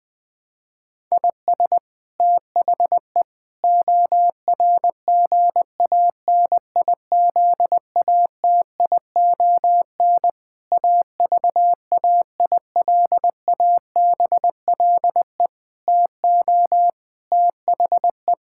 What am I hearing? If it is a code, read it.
Morse code, heard as a keyed tone, about 20 wpm, 700 Hz. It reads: IS THE ORGANIZATION AVAILABLE TO THE
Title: Sentences from Top 1000 Words - Review Part 5 - 20wpm